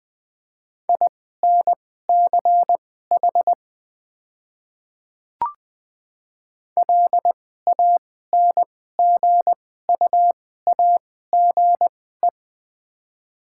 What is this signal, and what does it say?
Morse code, heard as a keyed tone, about 20 wpm, spaced as for 14 wpm, 700 Hz. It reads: INCH LANGUAGE
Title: 2x Character Spacing - Top 500 Words - 20wpm